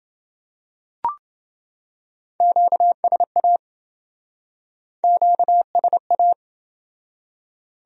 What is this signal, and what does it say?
Morse code, heard as a keyed tone, about 27 wpm, 700 Hz. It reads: QSA QSA